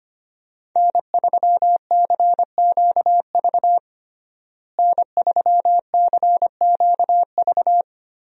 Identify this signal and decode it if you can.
Morse code, heard as a keyed tone, about 25 wpm, 700 Hz. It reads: N3CQV N3CQV
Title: USA Call Signs - 5 Characters - Repeated 2x - 25wpm